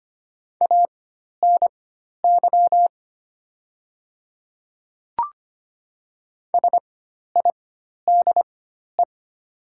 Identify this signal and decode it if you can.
Morse code, heard as a keyed tone, about 25 wpm, spaced as for 12 wpm, 700 Hz. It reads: ANY SIDE
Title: 4x Character Spacing - Top 200 Words - 25wpm